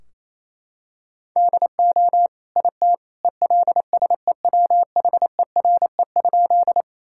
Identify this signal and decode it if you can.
Morse code, heard as a keyed tone, about 28 wpm, 700 Hz. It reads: DO IT ELSEWHERE?
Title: Sentences from Top 1600 Words 28wpm